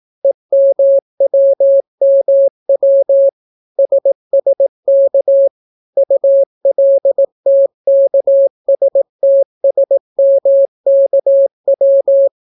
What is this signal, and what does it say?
Morse code, heard as a keyed tone, about 18 wpm, 550 Hz. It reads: EMWMW SSK ULTKSTSMKW